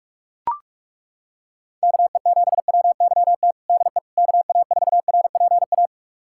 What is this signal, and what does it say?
Morse code, heard as a keyed tone, about 42 wpm, 700 Hz. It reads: KE7WYT DE KA4RPA
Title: USA Callsign DE Callsign 45wpm